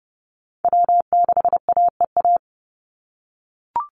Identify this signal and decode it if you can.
Morse code, heard as a keyed tone, about 30 wpm, 700 Hz. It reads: W6AEA
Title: Call Signs - 30wpm